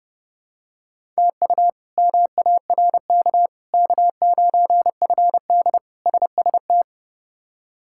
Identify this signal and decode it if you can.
Morse code, heard as a keyed tone, about 32 wpm, 700 Hz. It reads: TU MARK K9FD SST
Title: K1USN SST QSO Practice - 30wpm